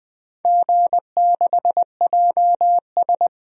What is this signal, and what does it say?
Morse code, heard as a keyed tone, about 20 wpm, 700 Hz. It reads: G6JS
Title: DXCC-100 Call Signs - 4 Characters - 20wpm